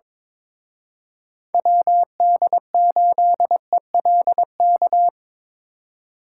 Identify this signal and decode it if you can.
Morse code, heard as a keyed tone, about 26 wpm, 700 Hz. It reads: WD8ELK
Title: Speed Racing - USA 2x3 Call Signs 22wpm